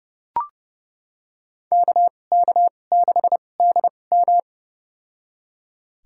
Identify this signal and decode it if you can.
Morse code, heard as a keyed tone, about 30 wpm, spaced as for 22 wpm, 700 Hz. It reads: KK6DM
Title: USA Call Signs - 5 Characters - 2x Character Spacing - 30wpm